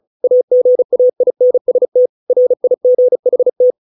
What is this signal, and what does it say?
Morse code, heard as a keyed tone, about 35 wpm, 500 Hz. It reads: AGAINST RIGHT